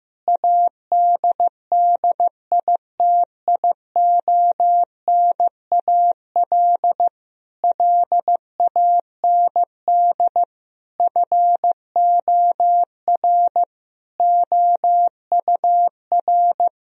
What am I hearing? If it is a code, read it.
Morse code, heard as a keyed tone, about 15 wpm, 700 Hz. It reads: ADDITIONAL LAND FOR OUR
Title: Sentences from Top 1000 Words - Review Part 4 - 15wpm